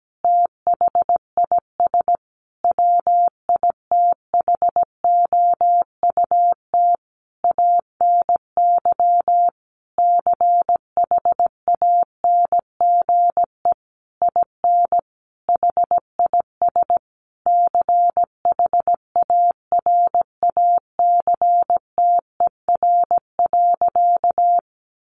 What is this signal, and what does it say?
Morse code, heard as a keyed tone, about 17 wpm, 700 Hz. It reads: THIS WITHOUT ANY CHANGE IN HIS CHARACTER.
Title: Sentences from Top 1200 Words 17wpm